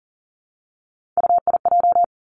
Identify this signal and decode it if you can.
Morse code, heard as a keyed tone, about 40 wpm, 690 Hz. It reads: UIJ
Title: Three Challenging Letter-Number Combinations 40wpm